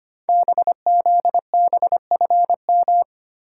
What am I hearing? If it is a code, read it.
Morse code, heard as a keyed tone, about 25 wpm, 700 Hz. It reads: BZBFM
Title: Code Groups - 5 Letters - 25wpm